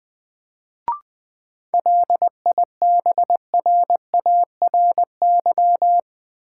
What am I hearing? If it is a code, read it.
Morse code, heard as a keyed tone, about 20 wpm, 700 Hz. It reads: LIBRARY
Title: Sentences from Top 600 Words - New Word Drill - 20wpm